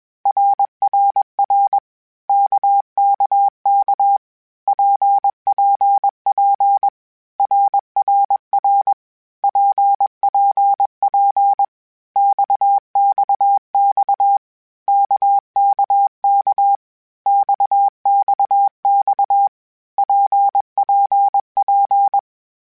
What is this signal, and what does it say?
Morse code, heard as a keyed tone, about 21 wpm, 800 Hz. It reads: RRR KKK PPP RRR PPP XXX KKK XXX PPP
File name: Groups@20 / Groups@20 mp3s / RKPX@20